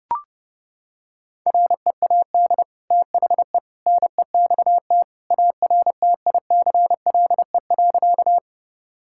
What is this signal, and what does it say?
Morse code, heard as a keyed tone, about 30 wpm, 700 Hz. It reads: READ THE NEXT ARTICLE.
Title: Sentences from Top 1000 Words - Review Part 5 - 30wpm